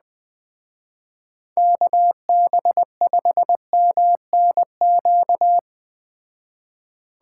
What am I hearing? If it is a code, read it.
Morse code, heard as a keyed tone, about 20 wpm, 700 Hz. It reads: KB5MNQ
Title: Speed Racing - USA 2x3 Call Signs 17wpm